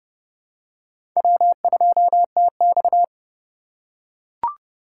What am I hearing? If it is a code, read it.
Morse code, heard as a keyed tone, about 30 wpm, 700 Hz. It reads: W2TX
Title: USA Call Signs - 4 Characters - 4x Character Spacing - 30wpm